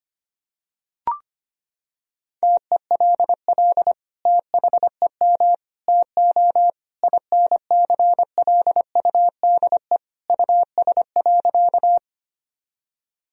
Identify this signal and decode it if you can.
Morse code, heard as a keyed tone, about 25 wpm, 700 Hz. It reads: TELL THEM TO INCLUDE US.
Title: Sentences from Top 1000 Words - Review Part 3 - 25wpm